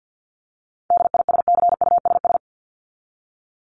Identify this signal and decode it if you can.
Morse code, heard as a keyed tone, about 49 wpm, 700 Hz. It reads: DISCUSS
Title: ED Suffix - Top 100 Words with Base Words 50wpm